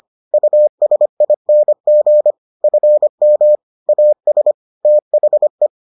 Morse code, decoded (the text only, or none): USING FM AS THE